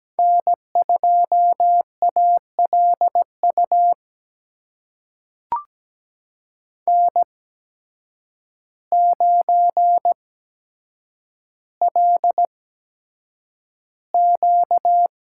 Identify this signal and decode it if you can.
Morse code, heard as a keyed tone, about 17 wpm, spaced as for 8 wpm, 700 Hz. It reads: N2ALU N9LQ
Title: USA Call Signs - 5 Characters - 8x Character Spacing - 17wpm